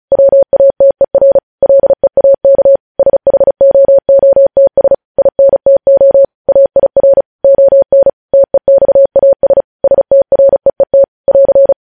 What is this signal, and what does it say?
Morse code, heard as a keyed tone, about 36 wpm, 550 Hz. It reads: WATER LEAK SHOOTS INTO AIR ON TEXAS STREET <AR>